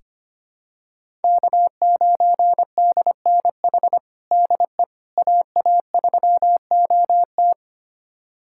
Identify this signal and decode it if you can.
Morse code, heard as a keyed tone, about 25 wpm, 700 Hz. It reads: K9DNH DE AA3OT
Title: USA Callsign DE Callsign - 5 Characters - 25wpm